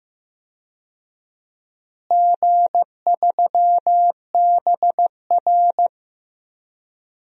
Morse code, heard as a keyed tone, about 15 wpm, 700 Hz. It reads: G3BR
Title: DXCC-100 Call Signs - 4 Characters - Repeated 2x - 15wpm